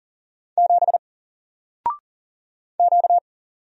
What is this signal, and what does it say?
Morse code, heard as a keyed tone, about 38 wpm, 700 Hz. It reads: 7 Q
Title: Single Letter-Number Alt 40wpm